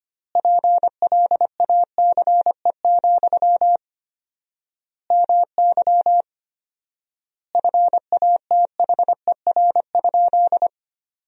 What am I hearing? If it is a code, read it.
Morse code, heard as a keyed tone, about 25 wpm, 700 Hz. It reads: PLACE, MY FATHER?
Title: 4x Word-Spacing - Sentences from Top 400 Words - 25wpm